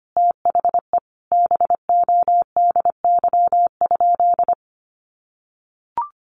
Morse code, heard as a keyed tone, about 25 wpm, 700 Hz. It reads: THE BODY?
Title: Sentences from Top 400 Words 25wpm